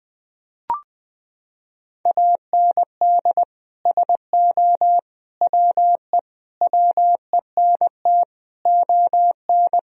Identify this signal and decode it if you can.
Morse code, heard as a keyed tone, about 20 wpm, 700 Hz. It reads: AND SO WE WENT ON
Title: Sentences from Top 1000 Words - Review Part 1 - 20wpm